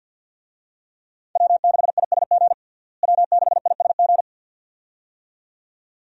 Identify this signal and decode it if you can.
Morse code, heard as a keyed tone, about 48 wpm, 700 Hz. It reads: W6ISG W6ISG
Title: USA Call Signs - 5 Characters - Repeated 2x - 50wpm